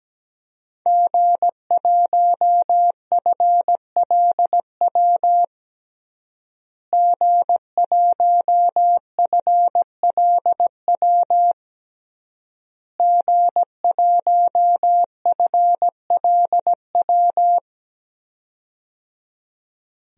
Morse code, heard as a keyed tone, about 17 wpm, 700 Hz. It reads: G1FLW G1FLW G1FLW
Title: DXCC-100 Call Signs - 5 Characters - Repeated 3x - 17wpm